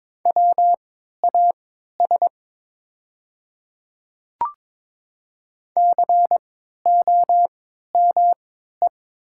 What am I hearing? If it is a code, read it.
Morse code, heard as a keyed tone, about 22 wpm, spaced as for 12 wpm, 690 Hz. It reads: WAS COME
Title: 3x Character Spacing - Top 100 Words - 22wpm